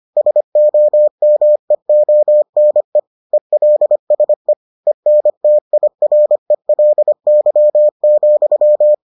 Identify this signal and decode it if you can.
Morse code, heard as a keyed tone, about 25 wpm, 600 Hz. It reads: SOMEONE ELSE ENTIRELY,